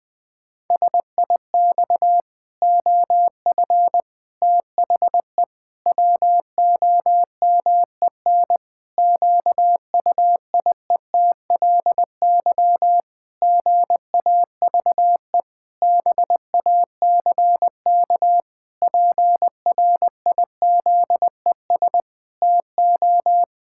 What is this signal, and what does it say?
Morse code, heard as a keyed tone, about 20 wpm, 700 Hz. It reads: SIX OF THE WOMEN QUIETLY GAVE BACK PRIZES TO
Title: Pangram Challenge v2 - 20wpm